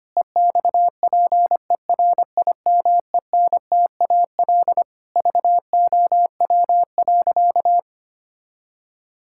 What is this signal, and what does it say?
Morse code, heard as a keyed tone, about 25 wpm, 700 Hz. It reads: EXPERIMENTAL VOW.
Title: Pangram Challenge v2 - 25wpm